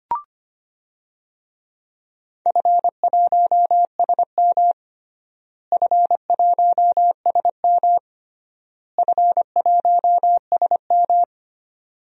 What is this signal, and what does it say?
Morse code, heard as a keyed tone, about 25 wpm, 700 Hz. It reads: F1SM F1SM F1SM
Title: DXCC-100 Call Signs - 4 Characters - Repeated 3x - 25wpm